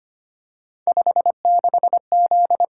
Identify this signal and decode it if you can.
Morse code, heard as a keyed tone, about 25 wpm, 700 Hz. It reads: 56Z
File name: Three Letter-Number Combinations - 25wpm